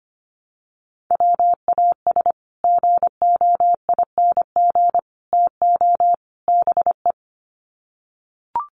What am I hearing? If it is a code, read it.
Morse code, heard as a keyed tone, about 25 wpm, 700 Hz. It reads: WAS GOING TO BE
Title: Sets of 4 Words 25wpm